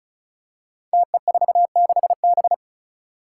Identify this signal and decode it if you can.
Morse code, heard as a keyed tone, about 36 wpm, 700 Hz. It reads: TE46B
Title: Five Challenging Letter-Number Combinations 35wpm